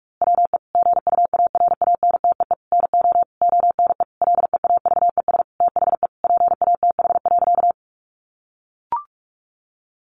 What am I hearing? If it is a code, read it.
Morse code, heard as a keyed tone, about 44 wpm, 700 Hz. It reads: WE GUARANTEE NO ONE LEAVES THE PATH.
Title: A Sound Of Thunder 45wpm